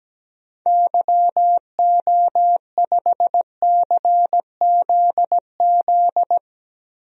Morse code, heard as a keyed tone, about 17 wpm, 700 Hz. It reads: YO5CZZ
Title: DXCC-100 Call Signs - 6 Characters - 17wpm